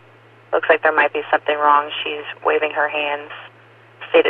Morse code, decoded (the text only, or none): none